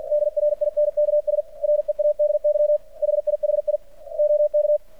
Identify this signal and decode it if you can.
Morse code, heard as a keyed tone, about 50 wpm, 600 Hz. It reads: DRINKS READY HIHI OK
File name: archived / CRSnet